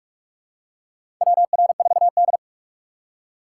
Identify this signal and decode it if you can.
Morse code, heard as a keyed tone, about 43 wpm, 690 Hz. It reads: WR4D
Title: USA Call Signs - 4 Characters - 8x Character Spacing - 45wpm